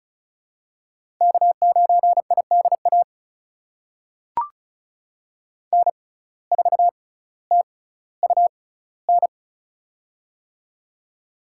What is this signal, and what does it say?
Morse code, heard as a keyed tone, about 34 wpm, spaced as for 18 wpm, 700 Hz. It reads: K9IDA N4TUN
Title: USA Call Signs - 6x Character Spacing - 35wpm